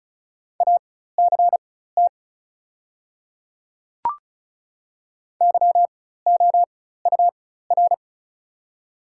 Morse code, heard as a keyed tone, about 33 wpm, spaced as for 16 wpm, 690 Hz. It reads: ACT YOUR
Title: 4x Character Spacing - Top 200 Words - 35wpm